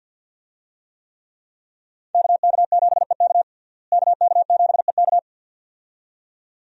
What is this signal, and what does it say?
Morse code, heard as a keyed tone, about 48 wpm, 700 Hz. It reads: KK7EK KK7EK
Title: USA Call Signs - 5 Characters - Repeated 2x - 50wpm